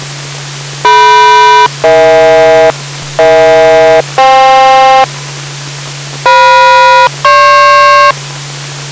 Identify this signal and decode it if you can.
Aircraft SELCAL, heard as two pairs of tones. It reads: DM-EH EH-FK FM-GP